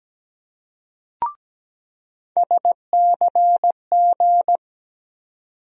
Morse code, heard as a keyed tone, about 17 wpm, 700 Hz. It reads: SCG
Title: USA Call Sign Postfixes - 3 Letters 17wpm